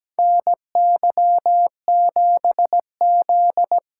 Morse code, heard as a keyed tone, about 17 wpm, 700 Hz. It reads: NY7Z
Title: K1USN SST QSO Practice - 17wpm